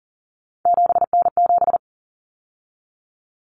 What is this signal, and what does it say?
Morse code, heard as a keyed tone, about 40 wpm, 700 Hz. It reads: 7N7
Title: Common Cut Numbers - 3-Digit 40wpm